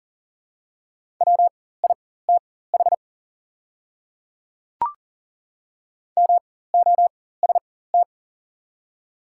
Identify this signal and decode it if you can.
Morse code, heard as a keyed tone, about 40 wpm, spaced as for 19 wpm, 700 Hz. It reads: WITH MOST
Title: 4x Character Spacing - Top 100 Words - 40wpm